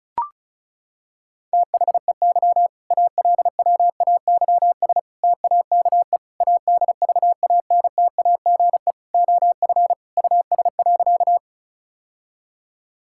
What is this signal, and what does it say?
Morse code, heard as a keyed tone, about 34 wpm, 700 Hz. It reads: THEY ALWAYS TAKE ADVANTAGE OF US.
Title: Sentences from Top 1000 Words - Review Part 6 - 35wpm